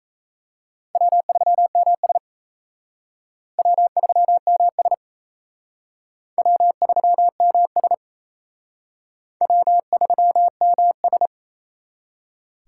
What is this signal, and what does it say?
Morse code, heard as a keyed tone, about 34 wpm, 700 Hz. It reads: W3MS W3MS W3MS W3MS
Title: Speed Racing - USA 1x2 Call Signs 28wpm